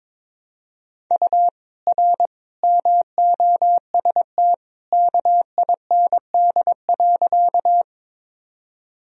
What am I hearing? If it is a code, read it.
Morse code, heard as a keyed tone, about 21 wpm, 700 Hz. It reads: U R MOST KIND.
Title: Sentences from Top 200 Words and Abbreviations 22wpm